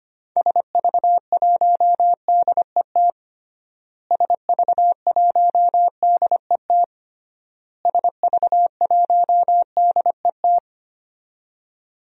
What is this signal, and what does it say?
Morse code, heard as a keyed tone, about 25 wpm, 700 Hz. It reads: SV1DET SV1DET SV1DET
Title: DXCC-100 Call Signs - 6 Characters - Repeated 3x - 25wpm